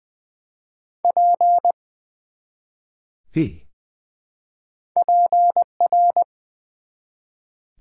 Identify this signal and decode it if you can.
Morse code, heard as a keyed tone, about 20 wpm, 700 Hz. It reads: P PR
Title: Build A Word - 1 Letter At A Time - 3 to 5-Letter Words - 20wpm